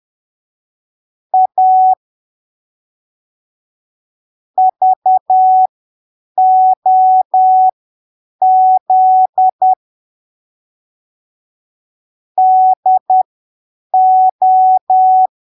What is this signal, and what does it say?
Morse code, heard as a keyed tone, about 10 wpm, spaced as for 7 wpm, 750 Hz. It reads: A VOZ DO
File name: arquivos / texto / texto 13 10ppm